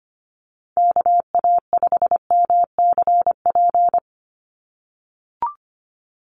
Morse code, heard as a keyed tone, about 25 wpm, 700 Hz. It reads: KA5MCP
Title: USA Call Signs 25wpm